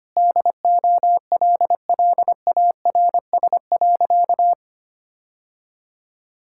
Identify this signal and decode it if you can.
Morse code, heard as a keyed tone, about 25 wpm, 700 Hz. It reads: DOLLARS.
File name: practice / 25wpm / Sentences with 9-Word Phrases - 25wpm